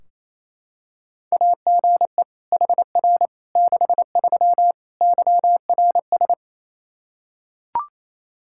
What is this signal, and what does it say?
Morse code, heard as a keyed tone, about 28 wpm, 700 Hz. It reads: AGE HR 63 YRS
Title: Practice QSO - Exchange 3 - More Info - 28wpm